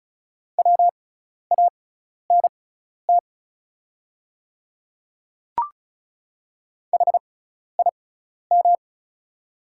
Morse code, heard as a keyed tone, about 36 wpm, spaced as for 12 wpm, 690 Hz. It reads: WANT HIM
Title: 6x Character Spacing - Top 200 Words - 35wpm